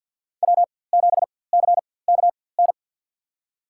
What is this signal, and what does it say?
Morse code, heard as a keyed tone, about 53 wpm, spaced as for 24 wpm, 700 Hz. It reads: W7CKN
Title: USA Call Signs - 4x Character Spacing - 50wpm